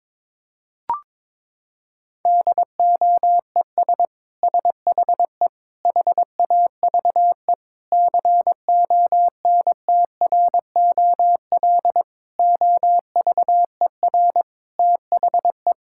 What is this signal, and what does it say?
Morse code, heard as a keyed tone, about 21 wpm, 700 Hz. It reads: DOES SHE HAVE CONTROL OVER THE
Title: Sentences from Top 1000 Words - Review Part 5 - 22wpm